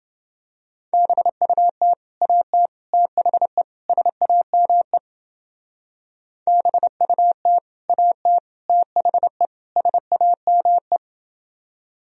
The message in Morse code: BUT AT THE SAME BUT AT THE SAME

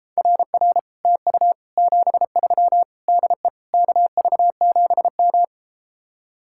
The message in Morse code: RR TU 73 DE KV7M

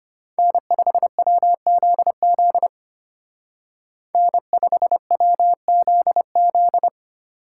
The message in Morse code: N5WZZ N5WZZ

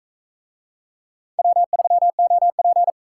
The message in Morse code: W3OP

